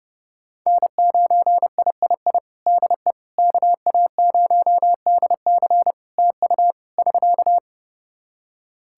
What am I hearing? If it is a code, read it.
Morse code, heard as a keyed tone, about 32 wpm, 700 Hz. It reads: N9III DE KA0DC TU <SK>
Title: Practice QSO - Exchange 4 - Ending - 30wpm